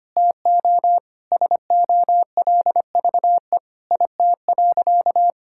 Morse code, heard as a keyed tone, about 25 wpm, 700 Hz. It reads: TO SOLVE IT.
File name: practice / 25wpm / Sentences with 3-Word Phrases - 25wpm